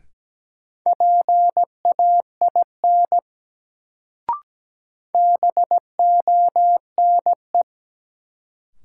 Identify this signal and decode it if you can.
Morse code, heard as a keyed tone, about 17 wpm, 700 Hz. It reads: PAIN BONE
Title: Sentences from Top 1600 Words - New Word Drill - 17wpm